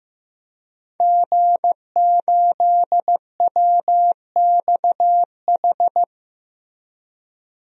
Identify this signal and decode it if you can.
Morse code, heard as a keyed tone, about 16 wpm, 700 Hz. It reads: G8WXH
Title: DXCC-100 Call Signs - 5 Characters - Repeated 2x - 15wpm